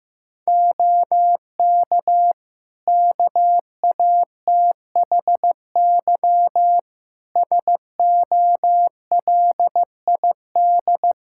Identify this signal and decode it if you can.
Morse code, heard as a keyed tone, about 16 wpm, 700 Hz. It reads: OK KATHY SOLID
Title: Practice QSO - Exchange 3 - More Info - 15wpm